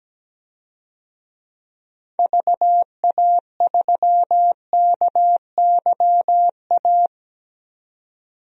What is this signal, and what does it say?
Morse code, heard as a keyed tone, about 17 wpm, 700 Hz. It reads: VA3KYA
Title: DXCC-100 Call Signs - 6 Characters - 17wpm